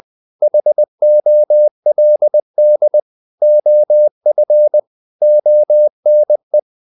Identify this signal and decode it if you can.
Morse code, heard as a keyed tone, about 20 wpm, 600 Hz. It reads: HOLD OF ONE
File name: the old man and the sea / the old man and the sea part 1 20wpm